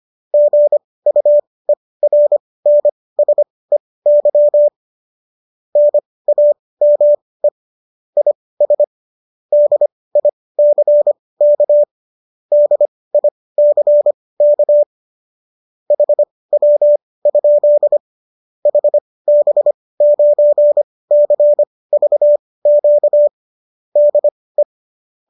Morse code, heard as a keyed tone, about 25 wpm, spaced as for 18 wpm, 590 Hz. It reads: GUERNSEY NAME IS DICK DICK HW? HB9CVQ DE